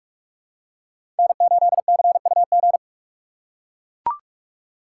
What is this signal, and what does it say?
Morse code, heard as a keyed tone, about 44 wpm, 700 Hz. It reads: N8CUG
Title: USA Call Signs - 2x Character Spacing - 45wpm